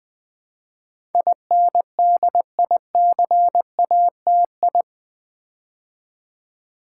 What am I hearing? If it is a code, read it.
Morse code, heard as a keyed tone, about 20 wpm, 700 Hz. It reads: INDICATI
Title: Build A Word - 4 Letters At A Time - 10-Letter Words - 20wpm